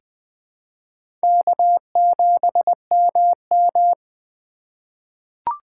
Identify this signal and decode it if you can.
Morse code, heard as a keyed tone, about 20 wpm, 700 Hz. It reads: K7MM
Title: USA Call Signs - 4 Characters - 4x Character Spacing - 20wpm